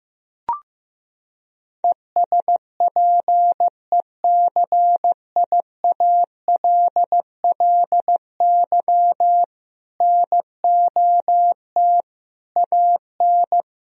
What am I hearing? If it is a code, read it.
Morse code, heard as a keyed tone, about 14 wpm, 700 Hz. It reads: ESPECIALLY NOT AN
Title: Sentences from Top 1000 Words - Review Part 4 - 15wpm